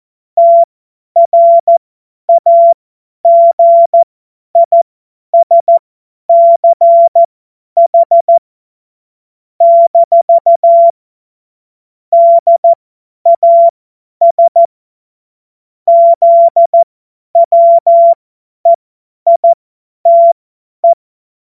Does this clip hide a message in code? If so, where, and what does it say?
Yes: Morse code from 0.4 to 20.9 s, heard as a keyed tone, about 14 wpm, spaced as for 10 wpm, 680 Hz. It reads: TRAGISCH - DAS ZWEITE